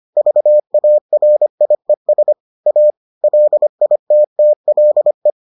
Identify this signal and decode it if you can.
Morse code, heard as a keyed tone, about 25 wpm, 600 Hz. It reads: VARIES A LITTLE